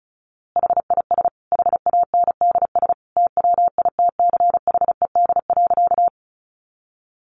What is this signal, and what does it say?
Morse code, heard as a keyed tone, about 34 wpm, 700 Hz. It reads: HIS HANDS TWITCHED.
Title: A Sound Of Thunder 35wpm